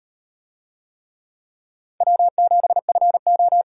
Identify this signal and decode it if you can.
Morse code, heard as a keyed tone, about 38 wpm, 700 Hz. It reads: W7FO